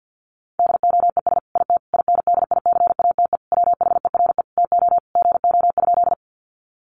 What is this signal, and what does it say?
Morse code, heard as a keyed tone, about 50 wpm, 700 Hz. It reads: DOES IT INDICATE WHERE TO GO?
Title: Sentences from Top 1300 Words 50wpm